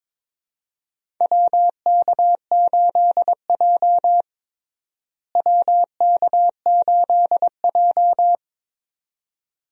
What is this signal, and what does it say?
Morse code, heard as a keyed tone, about 22 wpm, 700 Hz. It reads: WK8J WK8J
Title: USA Call Signs - 4 Characters - Repeated 2x - 22wpm